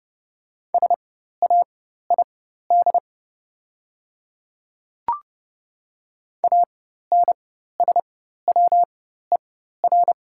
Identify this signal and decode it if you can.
Morse code, heard as a keyed tone, about 30 wpm, spaced as for 14 wpm, 700 Hz. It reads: SAID ANSWER
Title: 4x Character Spacing - Top 500 Words - 30wpm